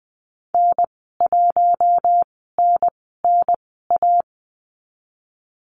Morse code, heard as a keyed tone, about 20 wpm, spaced as for 15 wpm, 700 Hz. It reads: N1NNA
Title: 062 Lesson 11g - Letter h - Call Signs 20wpm